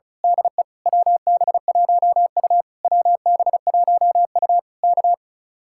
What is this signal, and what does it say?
Morse code, heard as a keyed tone, about 33 wpm, 700 Hz. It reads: DE WB1U WB1U K
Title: Speed-Racing Common QSO Phrases Encore - 35wpm